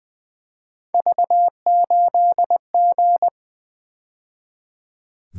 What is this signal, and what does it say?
Morse code, heard as a keyed tone, about 23 wpm, 700 Hz. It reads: V8G
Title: Speed-Racing - Three Letter-Number Combinations - 20wpm